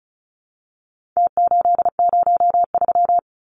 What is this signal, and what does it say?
Morse code, heard as a keyed tone, about 34 wpm, 700 Hz. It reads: T803
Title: Numbers - Contest Style - 4-Digit 35wpm